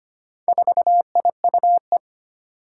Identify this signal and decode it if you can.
Morse code, heard as a keyed tone, about 25 wpm, 700 Hz. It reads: 4IUE
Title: Four Challenging Letter-Number Combinations 25wpm